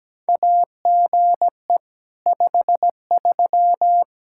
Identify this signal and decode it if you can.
Morse code, heard as a keyed tone, about 17 wpm, 700 Hz. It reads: AGE 53
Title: Practice QSO - Exchange 3 - More Info - 17wpm